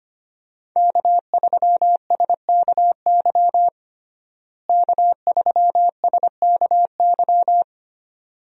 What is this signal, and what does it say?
Morse code, heard as a keyed tone, about 25 wpm, 700 Hz. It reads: K3SKY K3SKY
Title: USA Call Signs - 5 Characters - Repeated 2x - 25wpm